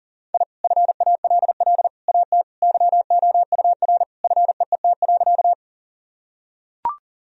Morse code, heard as a keyed tone, about 40 wpm, 700 Hz. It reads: I FALL AT YOUR FEET.
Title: Sentences from Top 1000 Words - Review Part 2 - 40wpm